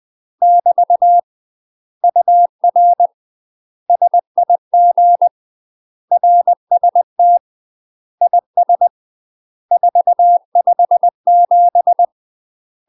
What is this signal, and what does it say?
Morse code, heard as a keyed tone, about 20 wpm, 700 Hz. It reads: <BT> UR SIG RST IS 457
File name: UK MASTER QSOs TRAINING/20WPM